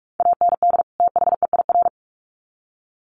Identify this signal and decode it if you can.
Morse code, heard as a keyed tone, about 44 wpm, 700 Hz. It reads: AND THEIR
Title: Sets of 2 Words 45wpm